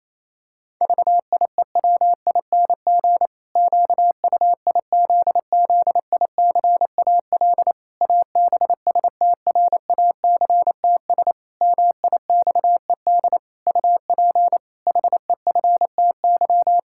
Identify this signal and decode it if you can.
Morse code, heard as a keyed tone, about 28 wpm, 700 Hz. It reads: VIEWING QUIZZICAL ABSTRACTS MIXED UP HEFTY
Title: Pangram Challenge v2 - 28wpm